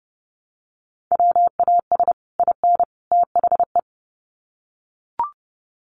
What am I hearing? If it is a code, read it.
Morse code, heard as a keyed tone, about 30 wpm, 700 Hz. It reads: WAS IN THE
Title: Sets of 3 Words 30wpm